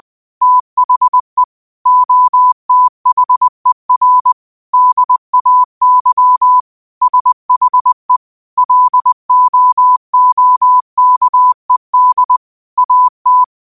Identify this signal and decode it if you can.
Morse code, heard as a keyed tone, about 20 wpm, 1000 Hz. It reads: THE OTHER DAY SHE LOOKED AT
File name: qotdfiles / 20wpm